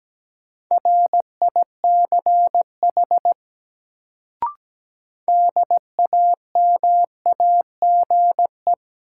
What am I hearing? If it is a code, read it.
Morse code, heard as a keyed tone, about 17 wpm, 700 Hz. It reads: RICH DAMAGE
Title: Sentences from Top 1100 Words - New Word Drill - 17wpm